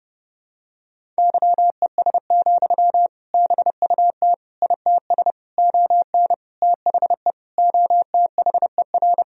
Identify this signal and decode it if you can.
Morse code, heard as a keyed tone, about 30 wpm, 700 Hz. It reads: YES, BUT ITS ON THE OTHER
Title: Sentences from Top 1000 Words - Review Part 1 - 30wpm